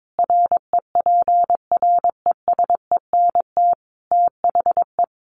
Morse code, heard as a keyed tone, about 22 wpm, 700 Hz. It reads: REPRESENT THE
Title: Sentences from Top 1200 Words 22wpm